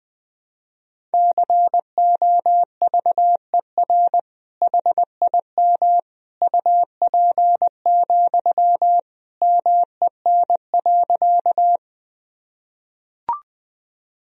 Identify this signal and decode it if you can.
Morse code, heard as a keyed tone, about 20 wpm, 700 Hz. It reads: COVER HIM UP, MEN.
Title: Sentences from Top 1000 Words - Review Part 1 - 20wpm